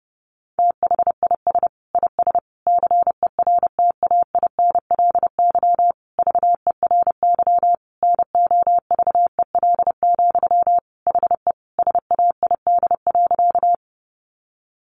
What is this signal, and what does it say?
Morse code, heard as a keyed tone, about 30 wpm, 700 Hz. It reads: THIS IS CERTAINLY VERY NOVEL, HE SAID.
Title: Sentences from Top 1300 Words 30wpm